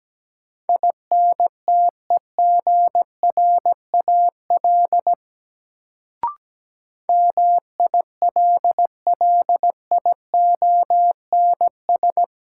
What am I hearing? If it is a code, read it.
Morse code, heard as a keyed tone, about 17 wpm, 700 Hz. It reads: INTEGRAL MILLIONS
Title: Sets of 5 Words - New Word Drill - 17wpm